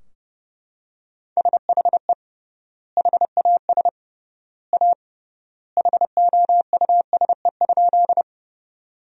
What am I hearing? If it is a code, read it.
Morse code, heard as a keyed tone, about 30 wpm, 700 Hz. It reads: SHE HAS A HOUSE?
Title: 3x Word-Spacing - Sentences from Top 200 Words - 30wpm